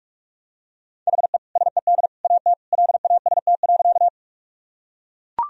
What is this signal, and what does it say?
Morse code, heard as a keyed tone, about 41 wpm, 700 Hz. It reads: HE SED AT LAST.
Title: Sentences from Top 300 Words and Abbreviations 45wpm